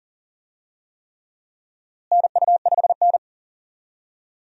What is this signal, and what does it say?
Morse code, heard as a keyed tone, about 40 wpm, 700 Hz. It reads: NU5N
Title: USA Call Signs - 4 Characters - Repeated 3x - 40wpm